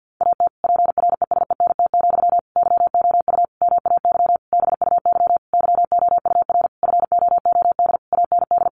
Morse code, heard as a keyed tone, about 50 wpm, 700 Hz. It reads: AT PRESENT, YOU MAY BUY YOUR FOOD AND
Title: Sentences from Top 800 Words 50wpm